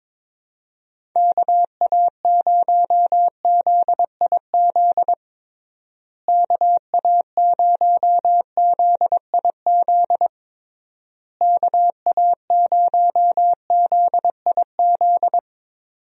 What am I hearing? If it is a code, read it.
Morse code, heard as a keyed tone, about 22 wpm, 700 Hz. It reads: KA0ZIZ KA0ZIZ KA0ZIZ
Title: USA Call Signs - 6 Characters - Repeated 3x - 22wpm